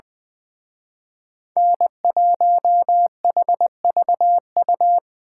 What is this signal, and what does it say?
Morse code, heard as a keyed tone, about 25 wpm, 700 Hz. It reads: N1HVU
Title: Speed Racing - USA 1x3 Call Signs 20wpm